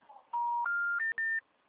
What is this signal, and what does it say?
Special information tone.